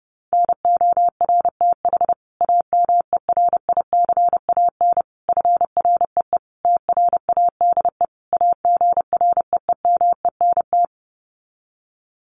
Morse code, heard as a keyed tone, about 30 wpm, 700 Hz. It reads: NORTH AMERICAN FREE TRADE AGREEMENT
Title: Sets of 5 Words Encore 30wpm